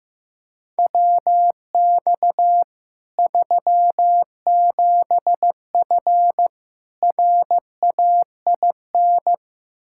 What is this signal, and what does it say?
Morse code, heard as a keyed tone, about 16 wpm, 700 Hz. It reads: WX 37F RAIN
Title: Practice QSO - Exchange 2 - Info - 15wpm